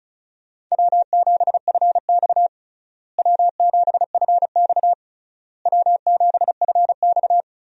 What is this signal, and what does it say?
Morse code, heard as a keyed tone, about 34 wpm, 700 Hz. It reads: W7FX W7FX W7FX
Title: USA Call Signs - 4 Characters - Repeated 3x - 35wpm